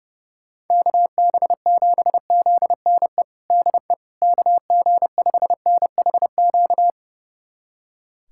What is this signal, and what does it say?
Morse code, heard as a keyed tone, about 32 wpm, 700 Hz. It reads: KB7ZNE DE KG5NHQ
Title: USA Callsign DE Callsign 30wpm